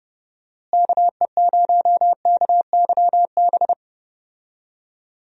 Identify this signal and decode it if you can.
Morse code, heard as a keyed tone, about 32 wpm, 700 Hz. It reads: KE0KYB
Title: USA Call Signs - 6 Characters - Repeated 3x - 30wpm